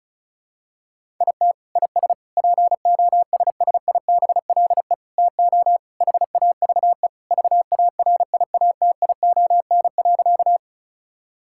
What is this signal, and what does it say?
Morse code, heard as a keyed tone, about 34 wpm, 700 Hz. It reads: IT IS POSSIBLE TO HAVE VARIATION.
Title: Sentences from Top 1600 Words 35wpm